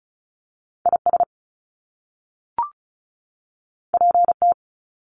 Morse code, heard as a keyed tone, about 34 wpm, 700 Hz. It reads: IS PT
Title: Top Two Letter Combinations in Words 35wpm